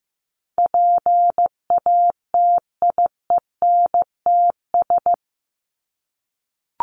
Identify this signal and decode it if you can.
Morse code, heard as a keyed tone, about 15 wpm, 700 Hz. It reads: PATIENTS
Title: ES:S Suffix - Top 100 Words with Base Words 15wpm